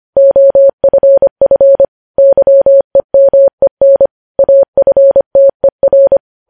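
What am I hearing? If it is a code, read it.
Morse code, heard as a keyed tone, about 25 wpm, 550 Hz. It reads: OFF YEMEN AFTER
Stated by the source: Morse code news headlines at 25 wpm.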